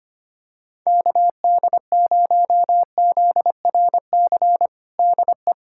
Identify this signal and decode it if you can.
Morse code, heard as a keyed tone, about 25 wpm, 700 Hz. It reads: KD0ZRC DE
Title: Practice QSO - Exchange 4 - Ending - 25wpm